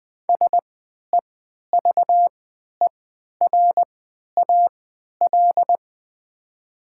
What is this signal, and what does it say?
Morse code, heard as a keyed tone, about 20 wpm, spaced as for 11 wpm, 700 Hz. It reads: SEVERAL
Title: 3x Character Spacing - Top 500 Words - 20wpm